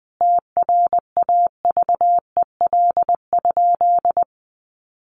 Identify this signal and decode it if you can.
Morse code, heard as a keyed tone, about 20 wpm, 700 Hz. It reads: TRAVEL?
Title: Sentences from Top 1400 Words 20wpm